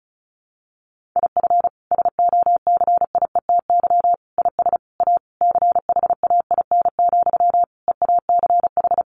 Morse code, heard as a keyed tone, about 34 wpm, 700 Hz. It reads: IF SOCIETY IS A CHAIN, EACH
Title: Sentences from Top 1500 Words 35wpm